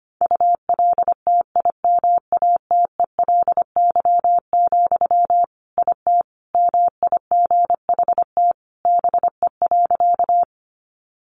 Morse code, heard as a keyed tone, about 25 wpm, 700 Hz. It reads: ULTIMATELY, IT MIGHT BE.
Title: Sentences from Top 1200 Words 25wpm